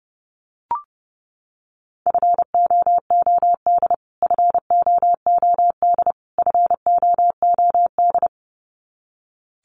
Morse code, heard as a keyed tone, about 30 wpm, 700 Hz. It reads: FOOD FOOD FOOD
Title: Top 500 Words Repeated 30wpm